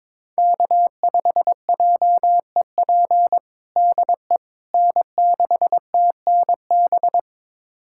Morse code, heard as a keyed tone, about 22 wpm, 700 Hz. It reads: K5JEP DE N6TNB
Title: USA Callsign DE Callsign - 5 Characters - 22wpm